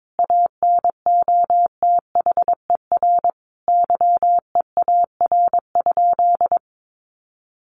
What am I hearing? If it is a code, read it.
Morse code, heard as a keyed tone, about 22 wpm, 700 Hz. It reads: ANOTHER YEAR?
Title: Sentences from Top 1500 Words 22wpm